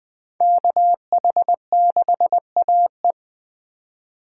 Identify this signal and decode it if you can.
Morse code, heard as a keyed tone, about 20 wpm, 700 Hz. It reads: KH6AE
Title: Speed Racing - USA 2x2 Call Signs 17wpm